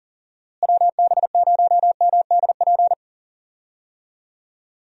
Morse code, heard as a keyed tone, about 40 wpm, 700 Hz. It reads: WB0MDP
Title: USA Call Signs - 6 Characters - Repeated 3x - 40wpm